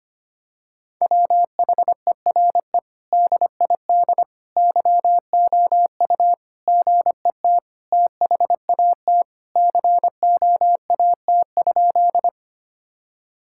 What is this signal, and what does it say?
Morse code, heard as a keyed tone, about 25 wpm, 700 Hz. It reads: WHERE DID YOU GET THAT COAT?
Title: Sentences from Top 1600 Words 25wpm